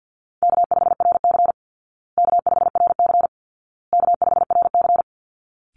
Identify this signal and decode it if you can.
Morse code, heard as a keyed tone, about 50 wpm, 700 Hz. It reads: K5RC K5RC K5RC
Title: Call Signs Repeated 2x - 50wpm